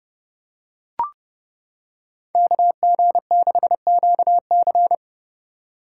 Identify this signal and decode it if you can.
Morse code, heard as a keyed tone, about 29 wpm, 700 Hz. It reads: KG6QC